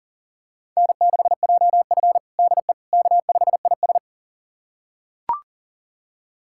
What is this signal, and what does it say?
Morse code, heard as a keyed tone, about 40 wpm, 700 Hz. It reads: N6JF DE K5IS
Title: USA Callsign DE Callsign - 4 Characters - 40wpm